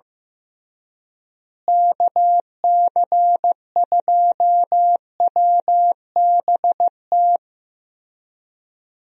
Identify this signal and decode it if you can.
Morse code, heard as a keyed tone, about 16 wpm, 700 Hz. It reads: KC2WBT
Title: USA Call Signs - 6 Characters - Repeated 3x - 15wpm